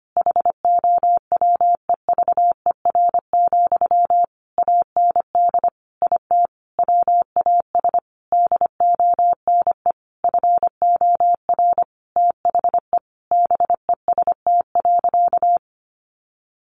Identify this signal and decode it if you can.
Morse code, heard as a keyed tone, about 25 wpm, 700 Hz. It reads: HOWEVER, AND IT WAS DONE FOR THE BEST.
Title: Sentences from Top 600 Words 25wpm